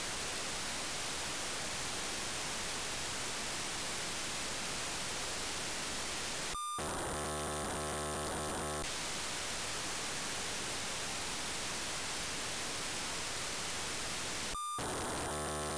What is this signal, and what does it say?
POCSAG pager data.
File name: wirelessness / pocsag ambulance